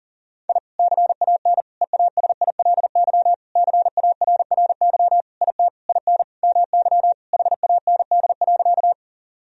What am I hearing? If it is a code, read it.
Morse code, heard as a keyed tone, about 37 wpm, 700 Hz. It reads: I CAN EASILY CARRY IT IN MY HAND.
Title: Sentences from Top 1000 Words - Review Part 1 - 40wpm